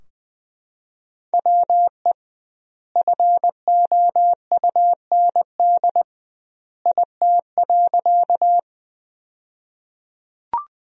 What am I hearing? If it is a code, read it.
Morse code, heard as a keyed tone, about 20 wpm, 700 Hz. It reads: WE FOUND IT.
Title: 2x Word-Spacing - Sentences from Top 300 Words - 20wpm